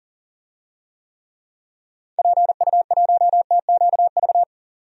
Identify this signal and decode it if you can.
Morse code, heard as a keyed tone, about 40 wpm, 700 Hz. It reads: PU1TQV
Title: DXCC-100 Call Signs - 6 Characters - 40wpm